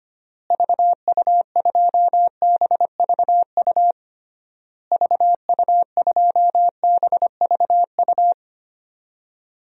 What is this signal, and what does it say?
Morse code, heard as a keyed tone, about 25 wpm, 700 Hz. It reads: VU2BVU VU2BVU
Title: DXCC-100 Call Signs - 6 Characters - Repeated 3x - 25wpm